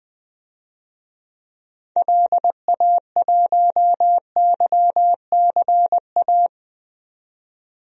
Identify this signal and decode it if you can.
Morse code, heard as a keyed tone, about 21 wpm, 700 Hz. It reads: LA1YCA
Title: DXCC-100 Call Signs - 6 Characters - 20wpm